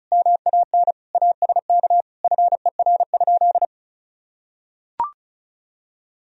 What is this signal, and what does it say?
Morse code, heard as a keyed tone, about 34 wpm, 700 Hz. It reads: MAN ASK FER?
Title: Sentences from Top 200 Words and Abbreviations 35wpm